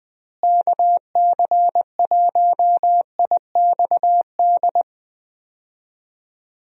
Morse code, heard as a keyed tone, about 21 wpm, 700 Hz. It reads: KC1IXD
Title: USA Call Signs - 6 Characters - Repeated 3x - 20wpm